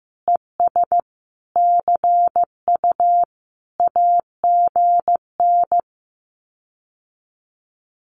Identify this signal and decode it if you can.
Morse code, heard as a keyed tone, about 15 wpm, 700 Hz. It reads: ES CU AGN
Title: Common QSO Phrases Encore 15wpm